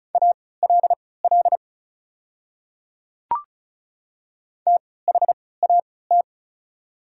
Morse code, heard as a keyed tone, about 36 wpm, spaced as for 20 wpm, 700 Hz. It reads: ALL THAT
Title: 3x Character Spacing - Top 100 Words - 35wpm